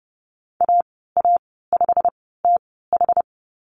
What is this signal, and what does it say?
Morse code, heard as a keyed tone, about 30 wpm, spaced as for 17 wpm, 700 Hz. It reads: AA5TH
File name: Lesson 15g - Number 5 - Call Signs 30wpm